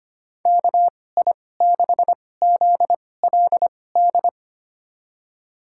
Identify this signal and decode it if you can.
Morse code, heard as a keyed tone, about 25 wpm, spaced as for 18 wpm, 700 Hz. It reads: KI6ZLD
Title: USA Call Signs - 6 Characters - 2x Character Spacing - 25wpm